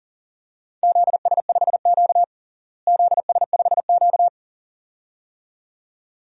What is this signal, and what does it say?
Morse code, heard as a keyed tone, about 40 wpm, 700 Hz. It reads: ZS5Q ZS5Q
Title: DXCC-100 Call Signs - 4 Characters - Repeated 2x - 40wpm